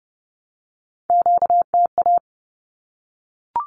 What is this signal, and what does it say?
Morse code, heard as a keyed tone, about 30 wpm, 700 Hz. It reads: QTA
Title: Q-Codes 30wpm